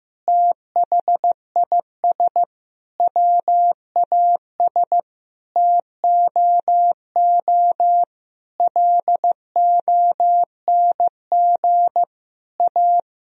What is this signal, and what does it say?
Morse code, heard as a keyed tone, about 16 wpm, 700 Hz. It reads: THIS WAS TOO LONG A
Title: Sentences from Top 1000 Words - Review Part 4 - 15wpm